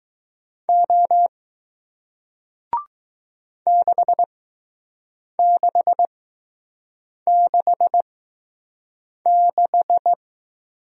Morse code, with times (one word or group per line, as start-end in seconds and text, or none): 0.7-1.3 O
3.7-10.1 6666